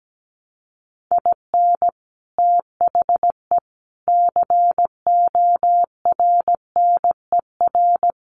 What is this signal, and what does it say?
Morse code, heard as a keyed tone, about 17 wpm, 700 Hz. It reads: IN THE CORNER
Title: Sets of 5 Words Encore 17wpm